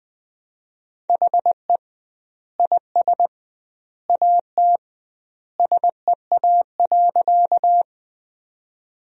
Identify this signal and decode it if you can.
Morse code, heard as a keyed tone, about 21 wpm, 700 Hz. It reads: HE IS AT SEA.
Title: 2x Word-Spacing - Sentences from Top 300 Words - 20wpm